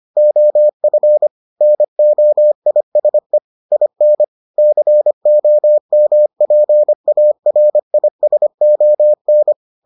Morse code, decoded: OF NOISE IN COMPARISON

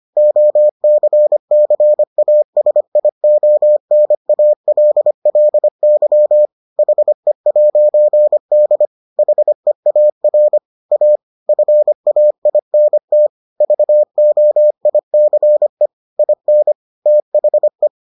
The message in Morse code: OCCASIONALLY HE'D HEAR A FAINT VOICE IN THE

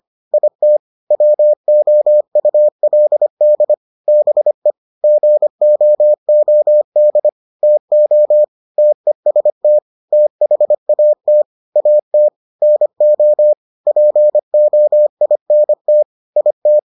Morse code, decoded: IT WOULD BE GOOD TO TEST THAT AT NO POINT IT

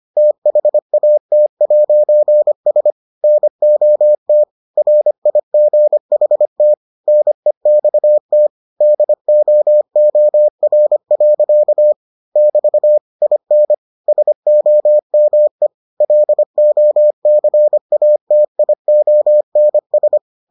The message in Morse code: THAT'S NOT RIGHT NEXT DOOR. <BT> IN SOME LOCATIONS

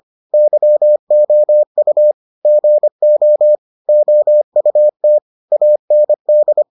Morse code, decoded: YOU GO OUT AND

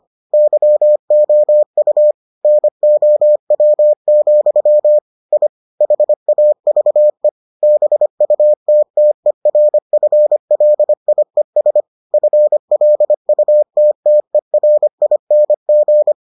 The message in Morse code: YOU NOW, I HAVE BUTTERFLIES FLUTTERING

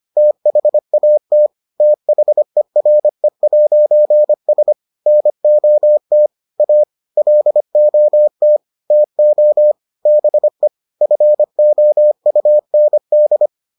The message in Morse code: THAT THERE'S NOT A LOT TO BE FOUND

